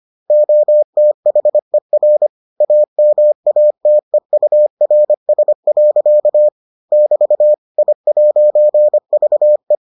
Morse code, heard as a keyed tone, about 25 wpm, 600 Hz. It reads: OTHER AMATEURS. <BT> I'VE